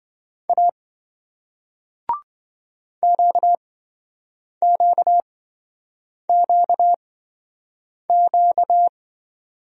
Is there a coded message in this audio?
Yes: Morse code from 0.5 to 8.9 s, heard as a keyed tone, about 27 wpm, spaced as for 7 wpm, 700 Hz. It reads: A QQQQ